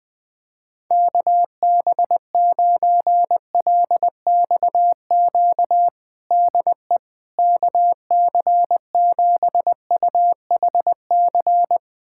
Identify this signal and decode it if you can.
Morse code, heard as a keyed tone, about 21 wpm, 700 Hz. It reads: KB9LXQ DE KC7UHC
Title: USA Callsign DE Callsign - 6 Characters - 20wpm